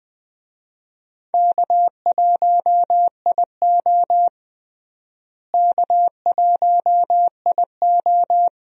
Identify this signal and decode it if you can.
Morse code, heard as a keyed tone, about 20 wpm, 700 Hz. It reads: K1IO K1IO
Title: USA Call Signs - 4 Characters - Repeated 2x - 20wpm